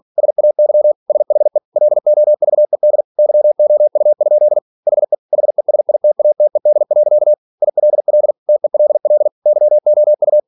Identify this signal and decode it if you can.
Morse code, heard as a keyed tone, about 46 wpm, 600 Hz. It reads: SAY SHE LOVED YOU? HE HESITATED. ILL TELL YOU
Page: Never 47wpm